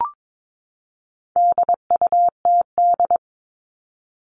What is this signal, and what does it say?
Morse code, heard as a keyed tone, about 22 wpm, 700 Hz. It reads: DUTD
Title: All Cut Numbers - 4-Digit 22wpm